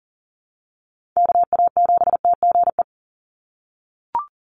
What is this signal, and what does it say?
Morse code, heard as a keyed tone, about 40 wpm, 700 Hz. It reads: KA7TGE
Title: USA Call Signs 40wpm